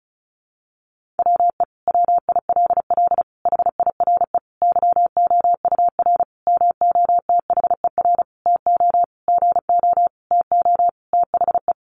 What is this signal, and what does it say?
Morse code, heard as a keyed tone, about 34 wpm, 700 Hz. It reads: WE WILL HIRE YOUR MOTHER TO GO TO THE
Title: Sentences from Top 1400 Words 35wpm